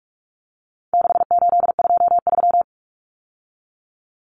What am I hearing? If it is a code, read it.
Morse code, heard as a keyed tone, about 44 wpm, 700 Hz. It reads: 6823
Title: Numbers - 4-Digit 45wpm